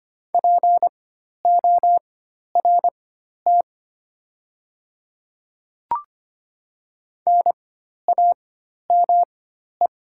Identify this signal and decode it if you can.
Morse code, heard as a keyed tone, about 25 wpm, spaced as for 12 wpm, 700 Hz. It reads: PORT NAME
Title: 4x Character Spacing - Top 200 Words - 25wpm